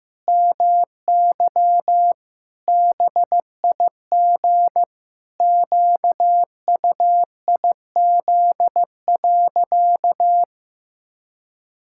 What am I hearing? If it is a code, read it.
Morse code, heard as a keyed tone, about 15 wpm, 700 Hz. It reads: MY BIG QUIZ.
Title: Pangram Challenge v2 - 15wpm